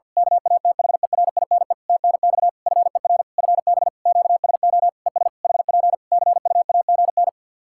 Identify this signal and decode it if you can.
Morse code, heard as a keyed tone, about 50 wpm, 700 Hz. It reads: KATHERINE TNX FER FB QSO ES HP CUAGN
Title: K1USN SST QSO Practice 50wpm